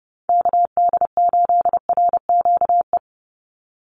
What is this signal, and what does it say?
Morse code, heard as a keyed tone, about 29 wpm, 700 Hz. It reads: KD8RQE
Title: Call Signs Repeated 2x - 30wpm